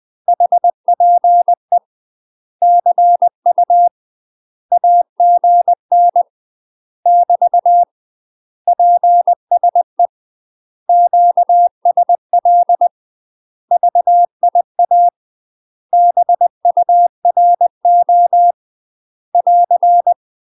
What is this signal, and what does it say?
Morse code, heard as a keyed tone, about 20 wpm, 700 Hz. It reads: HPE CU AGN <BT> PSE QSL VIA BURO <AR>
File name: UK MASTER QSOs TRAINING / 20WPM / UK MASTER QSOs 0070